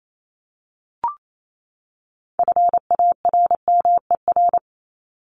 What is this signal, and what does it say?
Morse code, heard as a keyed tone, about 28 wpm, 700 Hz. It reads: FARMER
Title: Top 1751-2000 Words 28wpm